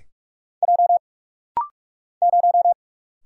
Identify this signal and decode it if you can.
Morse code, heard as a keyed tone, about 47 wpm, 700 Hz. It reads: J 0
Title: Single Letter-Number - Rapid-Fire 45wpm